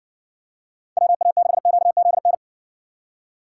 Morse code, heard as a keyed tone, about 60 wpm, 700 Hz. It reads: WA6QZN